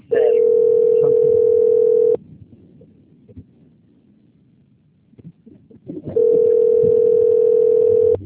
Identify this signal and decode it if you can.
Ringback tone (North America), 440 + 480 Hz.